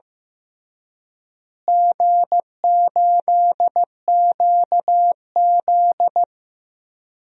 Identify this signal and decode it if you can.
Morse code, heard as a keyed tone, about 16 wpm, 700 Hz. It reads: G8QZ
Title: DXCC-100 Call Signs - 4 Characters - Repeated 3x - 15wpm